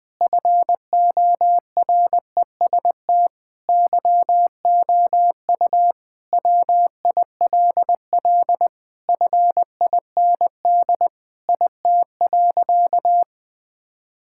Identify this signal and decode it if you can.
Morse code, heard as a keyed tone, about 21 wpm, 700 Hz. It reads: FOREST YOU WILL FIND IT.
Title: Sentences with 4-Word Phrases - 20wpm